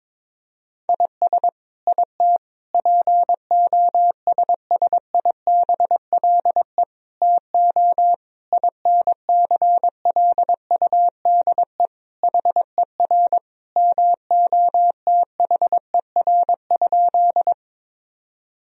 Morse code, heard as a keyed tone, about 22 wpm, 700 Hz. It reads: IS IT POSSIBLE TO INCLUDE HER MOTHER?
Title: Sentences from Top 1000 Words - Review Part 3 - 22wpm